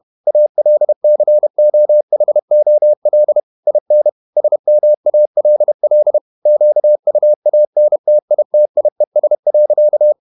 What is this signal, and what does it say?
Morse code, heard as a keyed tone, about 31 wpm, 600 Hz. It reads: ALCOHOL IN SMALL QUANTITIES.
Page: Never 31wpm | CW med Gnister